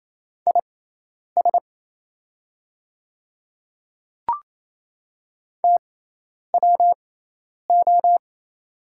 Morse code, heard as a keyed tone, about 28 wpm, spaced as for 10 wpm, 690 Hz. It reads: IS TWO
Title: 6x Character Spacing - Top 200 Words - 28wpm